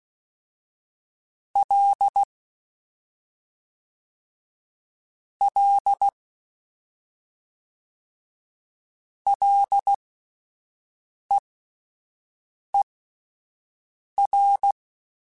Morse code, heard as a keyed tone, about 16 wpm, spaced as for 5 wpm, 800 Hz. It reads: L L LEER